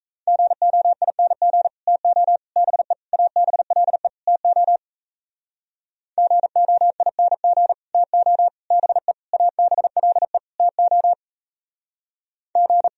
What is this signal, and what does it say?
Morse code, heard as a keyed tone, about 42 wpm, 700 Hz. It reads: GOING TO BE ABLE TO GOING TO BE ABLE TO G